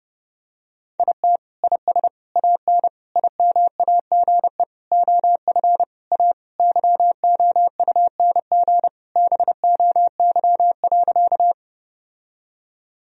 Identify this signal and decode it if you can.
Morse code, heard as a keyed tone, about 30 wpm, 700 Hz. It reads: IT IS AN IMAGE OF A YOUNG BOY.
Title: Sentences from Top 1000 Words - Review Part 4 - 30wpm